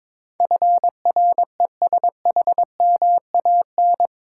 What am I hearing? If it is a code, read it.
Morse code, heard as a keyed tone, about 22 wpm, 700 Hz. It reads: FRESHMAN
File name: practice / 22wpm / Eight Letter Words - 22wpm